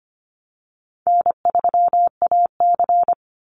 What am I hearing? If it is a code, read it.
Morse code, heard as a keyed tone, about 25 wpm, 700 Hz. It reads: N3AC
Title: Easy Call Signs Repeated 3x - 25wpm